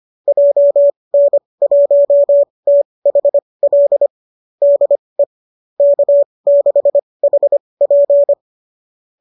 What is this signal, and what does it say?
Morse code, heard as a keyed tone, about 25 wpm, spaced as for 20 wpm, 590 Hz. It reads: JN1THL DE K6HP